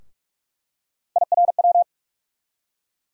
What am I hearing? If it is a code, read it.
Morse code, heard as a keyed tone, about 45 wpm, 700 Hz. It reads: IRW
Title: USA Call Sign Postfixes - 3 Letters 45wpm